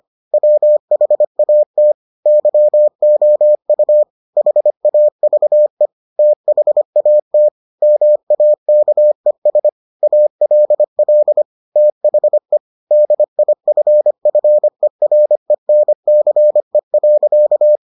Morse code, heard as a keyed tone, about 25 wpm, 600 Hz. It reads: WHAT YOU HAVE THAT MAKES ALL THE DIFFERENCE.